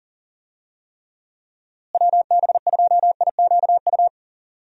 Morse code, heard as a keyed tone, about 40 wpm, 700 Hz. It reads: WB2IQU
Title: USA Call Signs - 6 Characters - Repeated 3x - 40wpm